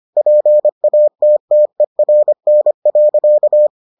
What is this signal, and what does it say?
Morse code, heard as a keyed tone, about 25 wpm, 600 Hz. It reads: PATTERN.